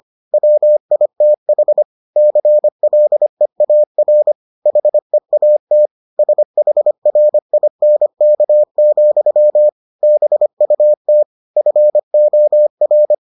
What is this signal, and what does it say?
Morse code, heard as a keyed tone, about 25 wpm, 600 Hz. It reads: WITH CLEAR HEAT SHRINK, BUT FOR